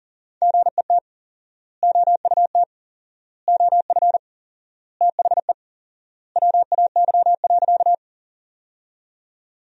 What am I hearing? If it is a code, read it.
Morse code, heard as a keyed tone, about 40 wpm, 700 Hz. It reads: GET OUT OF THE WAY.
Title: 4x Word-Spacing - Sentences from Top 200 Words - 40wpm